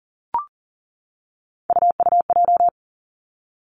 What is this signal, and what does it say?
Morse code, heard as a keyed tone, about 40 wpm, 700 Hz. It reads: UUJ
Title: Three Challenging Letter-Number Combinations 40wpm